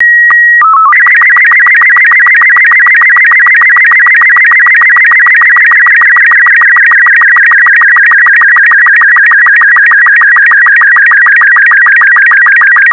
SSTV (slow-scan television), Robot 36.
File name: SSTV Robot36 Leo